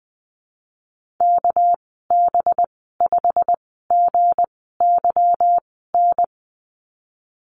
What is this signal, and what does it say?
Morse code, heard as a keyed tone, about 20 wpm, spaced as for 15 wpm, 700 Hz. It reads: KB5GYN
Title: 231 Lesson 37g - Letter x - Call Signs 20wpm